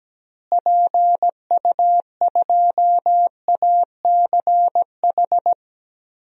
Morse code, heard as a keyed tone, about 17 wpm, 700 Hz. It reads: PU2ACH
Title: DXCC-100 Call Signs - 6 Characters - Repeated 3x - 17wpm